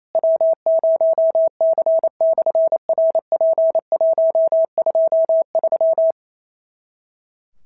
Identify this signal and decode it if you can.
Morse code, heard as a keyed tone, about 28 wpm, 650 Hz. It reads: W0C/RP123
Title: SOTA QSO Practice 28wpm